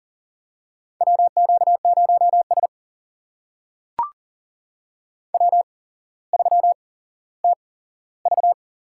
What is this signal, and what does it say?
Morse code, heard as a keyed tone, about 39 wpm, spaced as for 17 wpm, 700 Hz. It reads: WQ0S W3TV
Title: USA Call Signs - 8x Character Spacing - 40wpm